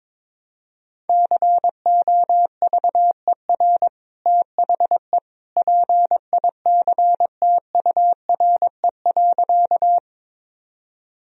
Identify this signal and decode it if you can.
Morse code, heard as a keyed tone, about 22 wpm, 700 Hz. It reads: COVER THE PICTURE.
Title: Sentences from Top 1000 Words - Review Part 1 - 22wpm